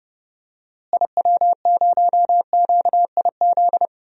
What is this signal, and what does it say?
Morse code, heard as a keyed tone, about 32 wpm, 700 Hz. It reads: IW0QIZ
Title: DXCC-100 Call Signs - 6 Characters - Repeated 3x - 30wpm